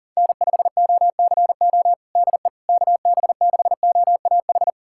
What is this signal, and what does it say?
Morse code, heard as a keyed tone, about 40 wpm, 700 Hz. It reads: N5OCO DE KB6OAH
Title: USA Callsign DE Callsign 40wpm